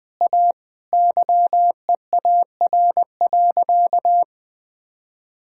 Morse code, heard as a keyed tone, about 20 wpm, 700 Hz. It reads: A YEAR.
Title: Sentences with 5-Word Phrases - 20wpm